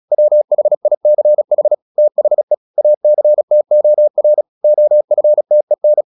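Morse code, heard as a keyed tone, about 36 wpm, 600 Hz. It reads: WHICH THE ACTOR OFTEN
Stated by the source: Morse practice at 36 wpm, text of Ken Follett's novel Never, from CW med Gnister.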